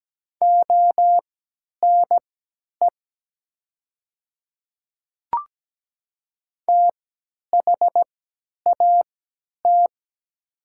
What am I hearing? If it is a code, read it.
Morse code, heard as a keyed tone, about 17 wpm, spaced as for 10 wpm, 700 Hz. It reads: ONE THAT
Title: 3x Character Spacing - Top 100 Words - 17wpm